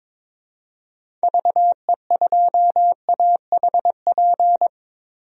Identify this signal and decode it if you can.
Morse code, heard as a keyed tone, about 22 wpm, 700 Hz. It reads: VE2AHP
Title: DXCC-100 Call Signs - 6 Characters - Repeated 2x - 22wpm